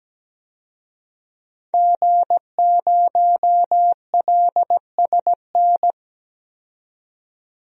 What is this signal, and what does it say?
Morse code, heard as a keyed tone, about 17 wpm, 700 Hz. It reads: G0LSN
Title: DXCC-100 Call Signs - 5 Characters - 17wpm